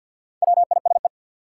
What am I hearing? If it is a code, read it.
Morse code, heard as a keyed tone, about 52 wpm, 700 Hz. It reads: WISE
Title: Sentences from Top 1600 Words - New Word Drill Alt - 50wpm